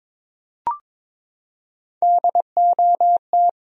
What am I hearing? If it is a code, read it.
Morse code, heard as a keyed tone, about 22 wpm, 700 Hz. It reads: DOT
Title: USA Common Names - Max 3 Letters - 22wpm